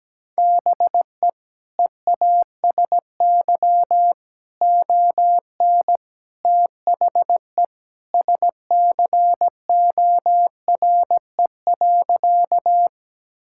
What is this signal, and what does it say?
Morse code, heard as a keyed tone, about 17 wpm, 700 Hz. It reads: BE EASY ON THE SCORE.
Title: Sentences from Top 1600 Words 17wpm